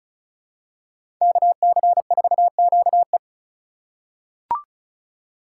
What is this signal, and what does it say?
Morse code, heard as a keyed tone, about 34 wpm, 700 Hz. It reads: KC4QE
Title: USA Call Signs - 5 Characters - 4x Character Spacing - 35wpm